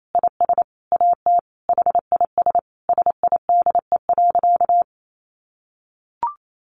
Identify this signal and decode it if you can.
Morse code, heard as a keyed tone, about 28 wpm, 700 Hz. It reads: IS AT HIS SIDE.
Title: Sentences from Top 1500 Words 28wpm